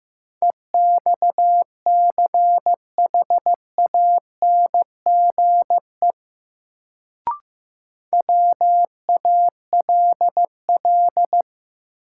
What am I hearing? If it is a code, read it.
Morse code, heard as a keyed tone, about 15 wpm, 700 Hz. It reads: EXCHANGE WALL
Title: Sentences from Top 1000 Words - Review Part 6 - New Word Drill - 15wpm